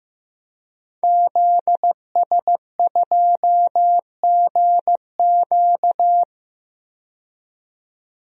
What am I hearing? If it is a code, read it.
Morse code, heard as a keyed tone, about 15 wpm, 700 Hz. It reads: ZS2GQ
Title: DXCC-100 Call Signs - 5 Characters - 15wpm